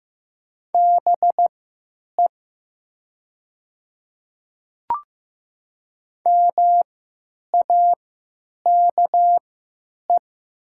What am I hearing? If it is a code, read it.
Morse code, heard as a keyed tone, about 15 wpm, spaced as for 9 wpm, 700 Hz. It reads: BE MAKE
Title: 3x Character Spacing - Top 300 Words - 15wpm